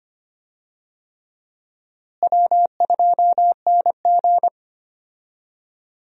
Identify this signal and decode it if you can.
Morse code, heard as a keyed tone, about 25 wpm, 700 Hz. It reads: W2NG
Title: USA Call Signs - 4 Characters - Repeated 3x - 25wpm